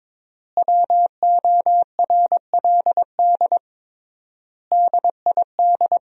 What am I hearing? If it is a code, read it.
Morse code, heard as a keyed tone, about 22 wpm, 700 Hz. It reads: WORLD DID
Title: 3x Word-Spacing - Sentences from Top 200 Words - 22wpm